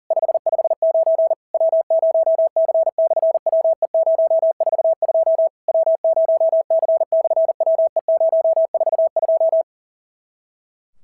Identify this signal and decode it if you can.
Morse code, heard as a keyed tone, about 40 wpm, 650 Hz. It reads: 559 W0C/WE042 W0C/WE042
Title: SOTA QSO Practice 40wpm